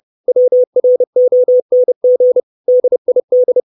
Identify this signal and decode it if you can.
Morse code, heard as a keyed tone, about 30 wpm, 500 Hz. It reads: WRONG DID
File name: 632 Words 30WPM